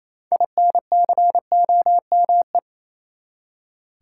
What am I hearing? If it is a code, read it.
Morse code, heard as a keyed tone, about 28 wpm, 700 Hz. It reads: INCOME
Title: Six Letter Words - 28wpm